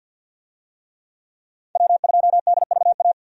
Morse code, heard as a keyed tone, about 46 wpm, 700 Hz. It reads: W2DVA
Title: USA Call Signs - 5 Characters - 50wpm